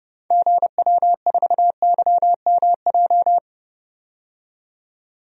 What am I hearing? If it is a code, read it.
Morse code, heard as a keyed tone, about 30 wpm, 690 Hz. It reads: GW4YMJ
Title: DXCC-100 Call Signs - 6 Characters - 30wpm